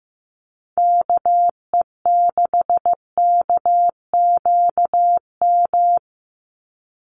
Morse code, heard as a keyed tone, about 15 wpm, 700 Hz. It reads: KE6KQM
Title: USA Call Signs 15wpm